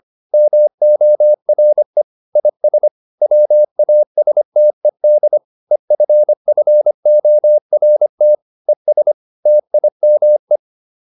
Morse code, heard as a keyed tone, about 25 wpm, 600 Hz. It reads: MORE IS WASTED EFFORT ES TIME